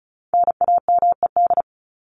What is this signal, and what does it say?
Morse code, heard as a keyed tone, about 34 wpm, 700 Hz. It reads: NAMED
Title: ED Suffix - Top 100 Words with Base Words 35wpm